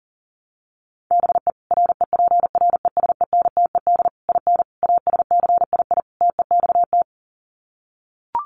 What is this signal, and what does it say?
Morse code, heard as a keyed tone, about 40 wpm, 700 Hz. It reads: BE REPRESENTED IN ASCII TEXT
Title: Sets of 5 Words 40wpm